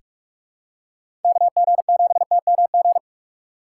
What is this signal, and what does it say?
Morse code, heard as a keyed tone, about 46 wpm, 700 Hz. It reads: KG7TMG
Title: USA Call Signs - 6 Characters - 3x Character Spacing - 45wpm